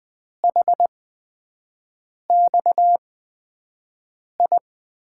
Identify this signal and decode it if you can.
Morse code, heard as a keyed tone, about 20 wpm, spaced as for 5 wpm, 700 Hz. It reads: HXI
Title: USA Call Signs - 6 Characters - 8x Character Spacing - 20wpm